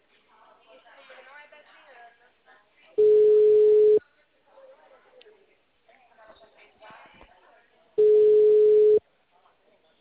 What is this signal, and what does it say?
Ringback tone (Europe, 425 Hz), with speech.